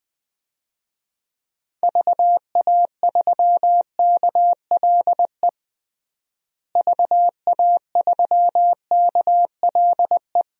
Morse code, heard as a keyed tone, about 20 wpm, 700 Hz. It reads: VA3KLE VA3KLE
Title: DXCC-100 Call Signs - 6 Characters - Repeated 3x - 20wpm